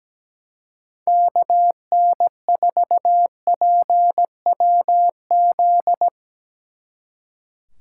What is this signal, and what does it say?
Morse code, heard as a keyed tone, about 19 wpm, 700 Hz. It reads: KN4PWZ
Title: Speed Racing - USA Call Signs 17wpm